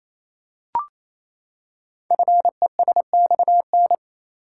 Morse code, heard as a keyed tone, about 29 wpm, 700 Hz. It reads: FESXN
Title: Code Groups - 5 Letters - 28wpm